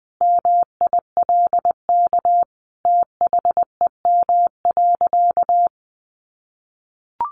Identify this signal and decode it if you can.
Morse code, heard as a keyed tone, about 20 wpm, 700 Hz. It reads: MILK THEM.
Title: Sentences from Top 1500 Words 20wpm